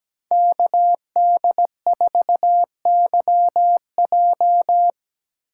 Morse code, heard as a keyed tone, about 20 wpm, 700 Hz. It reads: KD4YJ
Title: Speed Racing - USA 2x2 Call Signs 17wpm